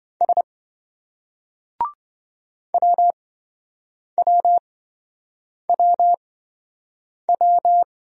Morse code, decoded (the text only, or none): S WWWW